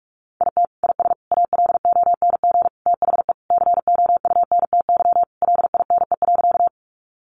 Morse code, heard as a keyed tone, about 44 wpm, 700 Hz. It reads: IT IS ALONG THE COUNTY LINE.
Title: Sentences from Top 1200 Words 45wpm